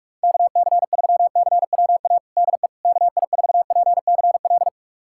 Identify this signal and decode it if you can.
Morse code, heard as a keyed tone, about 46 wpm, 700 Hz. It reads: KC3CWA DE KI4PCL
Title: USA Callsign DE Callsign 45wpm